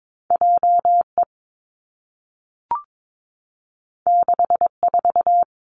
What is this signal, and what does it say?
Morse code, heard as a keyed tone, about 22 wpm, 700 Hz. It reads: J E 6 4
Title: Two Challenging Letter-Number Combinations 22wpm